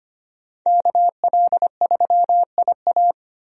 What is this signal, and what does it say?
Morse code, heard as a keyed tone, about 25 wpm, 700 Hz. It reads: KL3IA
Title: USA Call Signs - 5 Characters - Repeated 3x - 25wpm